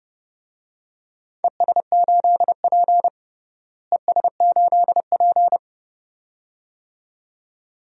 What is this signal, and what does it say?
Morse code, heard as a keyed tone, about 30 wpm, 700 Hz. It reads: ES8P ES8P
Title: DXCC-100 Call Signs - 4 Characters - Repeated 2x - 30wpm